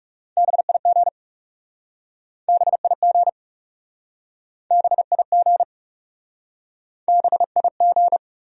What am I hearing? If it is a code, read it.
Morse code, heard as a keyed tone, about 40 wpm, 700 Hz. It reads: BIG BIG BIG BIG